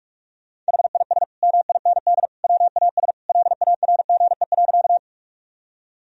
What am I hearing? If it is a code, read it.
Morse code, heard as a keyed tone, about 42 wpm, 700 Hz. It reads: HIS MIND WAS LARGE.
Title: Sentences from Top 1000 Words - Review Part 1 - 45wpm